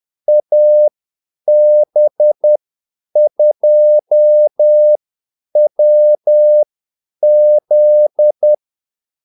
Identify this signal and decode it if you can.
Morse code, heard as a keyed tone, about 10 wpm, spaced as for 8 wpm, 600 Hz. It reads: AB2WZ
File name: Callsigns-10-8wpm